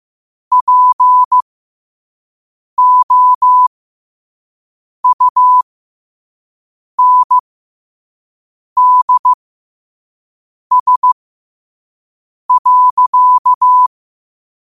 Morse code, heard as a keyed tone, about 15 wpm, spaced as for 5 wpm, 1000 Hz. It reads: POUNDS.
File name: qotdfiles / 5wpm